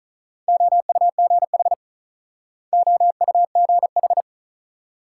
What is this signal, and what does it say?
Morse code, heard as a keyed tone, about 45 wpm, 700 Hz. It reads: OUGH OUGH